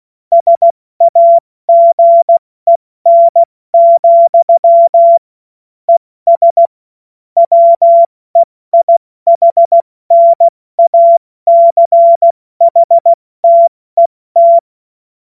Morse code, heard as a keyed tone, about 16 wpm, 680 Hz. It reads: SAGEN, ES WEIHNACHTET
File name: Morsekurs / Witze / Witz 31. Januar 2023 70 F80